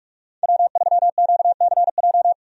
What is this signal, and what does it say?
Morse code, heard as a keyed tone, about 46 wpm, 700 Hz. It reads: W3QCJ
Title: USA Call Signs - 5 Characters - 45wpm